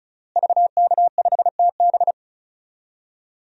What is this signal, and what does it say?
Morse code, heard as a keyed tone, about 36 wpm, 700 Hz. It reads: VK5TB
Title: DXCC-100 Call Signs - 5 Characters - Repeated 3x - 35wpm